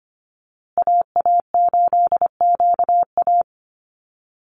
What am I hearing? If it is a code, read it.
Morse code, heard as a keyed tone, about 25 wpm, 700 Hz. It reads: AA8QA
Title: USA Call Signs Repeated 3x 25wpm